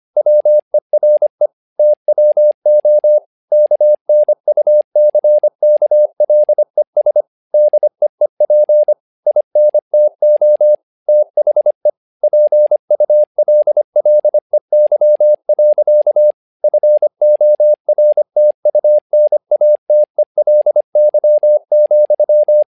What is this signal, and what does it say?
Morse code, heard as a keyed tone, about 25 wpm, 600 Hz. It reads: WERE TWO KNUCKLES DEEP INTO THE PULLEY. FORTUNATELY,
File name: learning center / tower trouble / tower trouble 25wpm